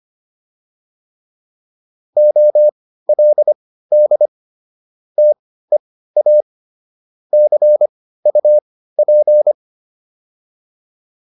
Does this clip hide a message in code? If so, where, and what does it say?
Yes: Morse code from 2.2 to 9.5 s, heard as a keyed tone, about 25 wpm, spaced as for 15 wpm, 600 Hz. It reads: OLD TEA CUP